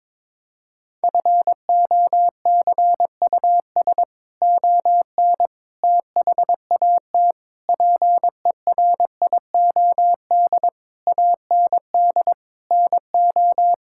Morse code, heard as a keyed tone, about 22 wpm, 700 Hz. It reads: FOCUS ON THAT PERIOD AND NO
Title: Sentences from Top 1000 Words - Review Part 5 - 22wpm